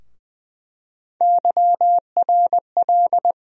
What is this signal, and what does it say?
Morse code, heard as a keyed tone, about 21 wpm, 700 Hz. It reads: YRL
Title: USA Call Sign Postfixes - 3 Letters 20wpm